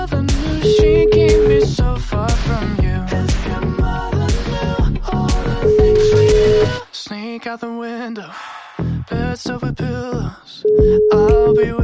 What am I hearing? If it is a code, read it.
Ringback tone (Europe, 425 Hz), with speech.